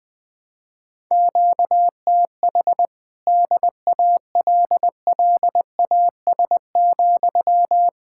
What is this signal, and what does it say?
Morse code, heard as a keyed tone, about 20 wpm, 700 Hz. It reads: QTH DALLAS,
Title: Speed-Racing Common QSO Phrases - 20wpm